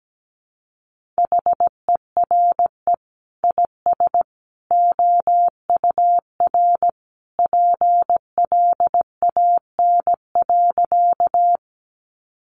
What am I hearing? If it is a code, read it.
Morse code, heard as a keyed tone, about 17 wpm, 700 Hz. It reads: HERE IS OUR PLAN.
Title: Sentences from Top 500 Words 17wpm